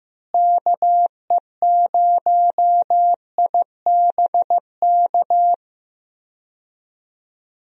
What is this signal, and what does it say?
Morse code, heard as a keyed tone, about 16 wpm, 700 Hz. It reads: KE0IBK
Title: USA Call Signs - 6 Characters - Repeated 2x - 15wpm